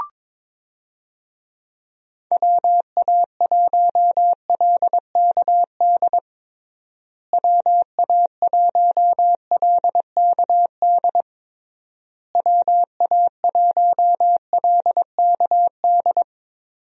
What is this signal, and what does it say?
Morse code, heard as a keyed tone, about 22 wpm, 700 Hz. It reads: WA1LKD WA1LKD WA1LKD
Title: USA Call Signs - 6 Characters - Repeated 3x - 22wpm